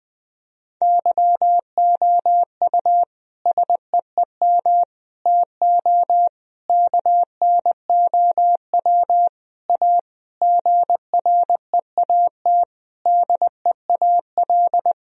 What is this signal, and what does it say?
Morse code, heard as a keyed tone, about 21 wpm, 700 Hz. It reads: YOU SEEM TO KNOW A GREAT DEAL
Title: Sentences from Top 1000 Words - Review Part 1 - 20wpm